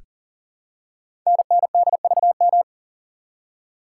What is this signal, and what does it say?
Morse code, heard as a keyed tone, about 41 wpm, 690 Hz. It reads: NNDVM
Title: Code Groups - 5 Letters - 40wpm